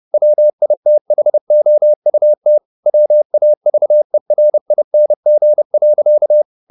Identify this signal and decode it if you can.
Morse code, heard as a keyed tone, about 30 wpm, 600 Hz. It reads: WITHOUT WAVERING.